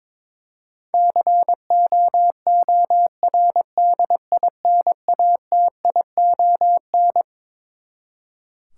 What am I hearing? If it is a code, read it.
Morse code, heard as a keyed tone, about 22 wpm, 700 Hz. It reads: COORDINATION
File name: Twelve Letter Words - 22wpm